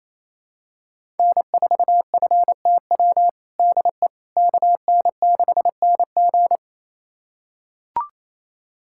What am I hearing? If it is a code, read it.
Morse code, heard as a keyed tone, about 28 wpm, 700 Hz. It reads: N4FTW DE KN6NG
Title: USA Callsign DE Callsign - 5 Characters - 28wpm